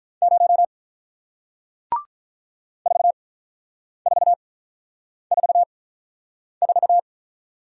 Morse code, heard as a keyed tone, about 44 wpm, spaced as for 9 wpm, 700 Hz. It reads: 0 4444